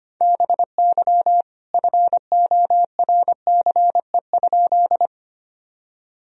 Morse code, heard as a keyed tone, about 25 wpm, 700 Hz. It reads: BY FORCE?
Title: Sentences from Top 1000 Words - Review Part 3 - 25wpm